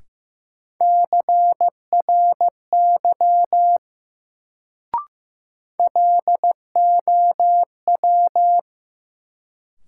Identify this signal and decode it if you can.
Morse code, heard as a keyed tone, about 15 wpm, 700 Hz. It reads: CRY LOW
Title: Sentences from Top 500 Words - New Word Drill - 15wpm